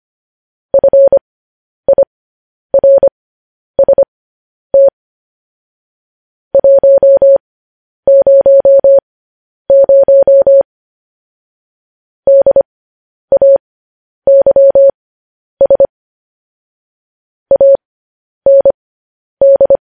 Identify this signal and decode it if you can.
Morse code, heard as a keyed tone, about 25 wpm, spaced as for 10 wpm, 550 Hz. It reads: FIRST 100 DAYS AND
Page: news headlines in morse code at 10 wpm